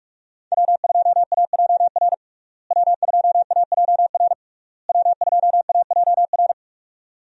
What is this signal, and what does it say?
Morse code, heard as a keyed tone, about 41 wpm, 700 Hz. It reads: W2AJR W2AJR W2AJR
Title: USA Call Signs - 5 Characters - Repeated 3x - 45wpm